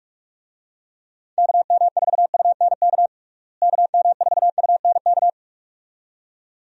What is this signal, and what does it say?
Morse code, heard as a keyed tone, about 46 wpm, 700 Hz. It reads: KM4UNK KM4UNK
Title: USA Call Signs - 6 Characters - Repeated 2x - 45wpm